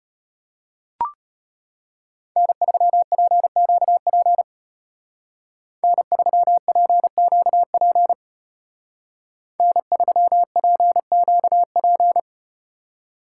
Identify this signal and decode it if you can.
Morse code, heard as a keyed tone, about 34 wpm, 700 Hz. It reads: N3PQP N3PQP N3PQP